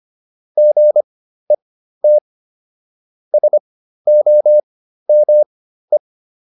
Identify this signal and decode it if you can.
Morse code, heard as a keyed tone, about 25 wpm, spaced as for 13 wpm, 600 Hz. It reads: GET SOME
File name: SS109 13